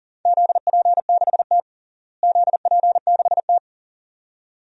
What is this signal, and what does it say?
Morse code, heard as a keyed tone, about 37 wpm, 700 Hz. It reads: ZP6T ZP6T
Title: DXCC-100 Call Signs - 4 Characters - Repeated 2x - 40wpm